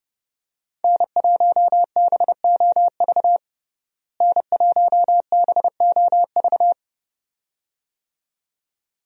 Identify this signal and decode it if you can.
Morse code, heard as a keyed tone, about 30 wpm, 700 Hz. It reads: N1BOV N1BOV
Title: USA Call Signs - 5 Characters - Repeated 2x - 30wpm